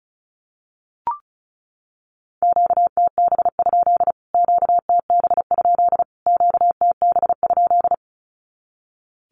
Morse code, heard as a keyed tone, about 34 wpm, 700 Hz. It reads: QTB? QTB? QTB?
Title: Q-Codes Repeated 3x 35wpm